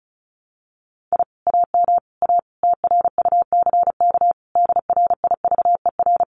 Morse code, heard as a keyed tone, about 34 wpm, 700 Hz. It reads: I AM A TRUCK DRIVER
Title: Common QSO Phrases Encore 35wpm